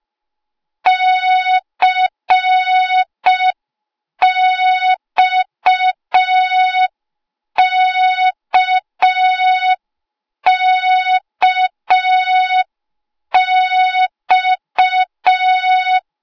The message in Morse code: CXKKX